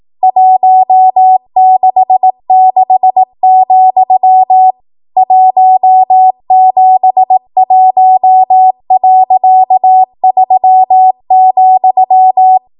Morse code, heard as a keyed tone, about 18 wpm, 750 Hz. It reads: 166, 171.3,
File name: Archive / 18 WPM / 18WPM